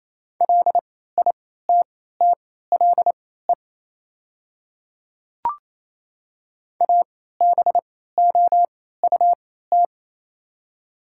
Morse code, heard as a keyed tone, about 28 wpm, spaced as for 16 wpm, 700 Hz. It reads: LITTLE ABOUT
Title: 3x Character Spacing - Top 300 Words - 28wpm